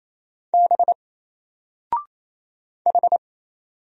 Morse code, heard as a keyed tone, about 28 wpm, 700 Hz. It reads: B H
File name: Single-Letter-Number-28wpm